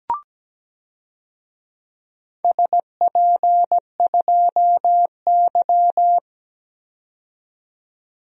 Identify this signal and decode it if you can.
Morse code, heard as a keyed tone, about 17 wpm, 700 Hz. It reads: SP2Y
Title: DXCC-100 Call Signs - 4 Characters - 17wpm